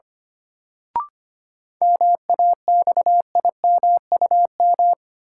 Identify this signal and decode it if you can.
Morse code, heard as a keyed tone, about 25 wpm, 700 Hz. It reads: MAXIMUM
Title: Sentences from Top 1300 Words - New Word Drill Alt - 25wpm